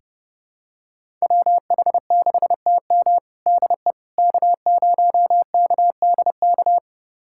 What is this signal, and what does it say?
Morse code, heard as a keyed tone, about 30 wpm, 700 Hz. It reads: WH6TM DE K0KDK
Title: USA Callsign DE Callsign - 5 Characters - 30wpm